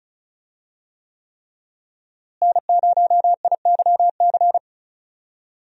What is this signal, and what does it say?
Morse code, heard as a keyed tone, about 34 wpm, 700 Hz. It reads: N0IYC
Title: USA Call Signs - 5 Characters - Repeated 3x - 35wpm